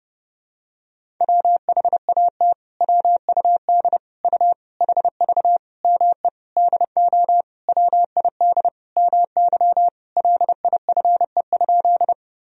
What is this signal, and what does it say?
Morse code, heard as a keyed tone, about 30 wpm, 700 Hz. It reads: WHAT WUD U HV ME DO WID MY LIFE?
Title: Sentences from Top 300 Words and Abbreviations 30wpm